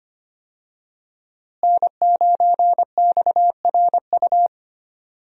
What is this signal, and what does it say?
Morse code, heard as a keyed tone, about 25 wpm, 700 Hz. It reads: N9XRU
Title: USA Call Signs - 5 Characters - Repeated 3x - 25wpm